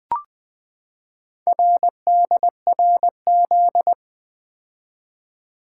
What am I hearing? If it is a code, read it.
Morse code, heard as a keyed tone, about 20 wpm, 700 Hz. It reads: RDRZ
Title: Four Letter-Number Combinations - 20wpm